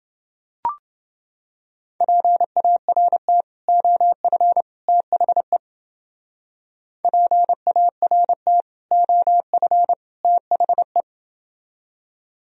Morse code, heard as a keyed tone, about 31 wpm, 700 Hz. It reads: PART OF THE PART OF THE